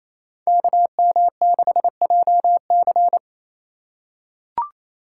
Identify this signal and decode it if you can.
Morse code, heard as a keyed tone, about 28 wpm, 700 Hz. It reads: KM6JC
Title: Code Groups - 5 Letter:Numbers - 28wpm